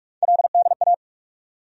Morse code, heard as a keyed tone, about 42 wpm, 700 Hz. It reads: LDA
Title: USA Call Sign Postfixes - 3 Letters 45wpm